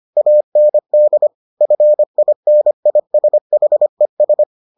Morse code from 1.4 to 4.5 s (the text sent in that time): FINISHES